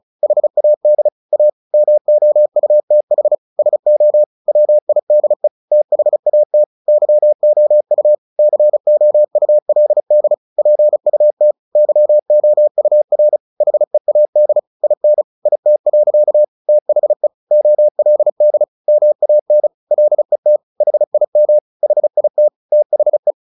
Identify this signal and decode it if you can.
Morse code, heard as a keyed tone, about 34 wpm, 600 Hz. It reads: HAD A MOUTH SO WIDE THAT YOU COULD PUT YOUR HEAD IN IT. THE OLD MAN LET HIM HIT THE
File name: the old man and the sea 2 35wpm